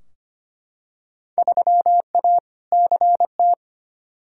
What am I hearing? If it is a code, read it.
Morse code, heard as a keyed tone, about 25 wpm, 700 Hz. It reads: 3A CT
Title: Field Day Exchanges - 25wpm